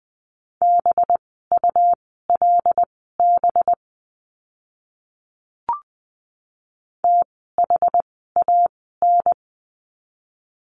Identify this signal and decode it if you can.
Morse code, heard as a keyed tone, about 20 wpm, spaced as for 15 wpm, 700 Hz. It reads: BULB THAN
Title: 185 Lesson 30f - Letter b - Words 20wpm